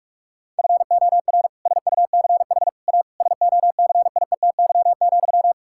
Morse code, heard as a keyed tone, about 41 wpm, 700 Hz. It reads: FOR SUCH A SOCIETY,
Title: Sentences from Top 1000 Words - Review Part 6 - 45wpm